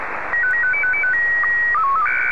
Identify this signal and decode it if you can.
SSTV (slow-scan television), Scottie 1.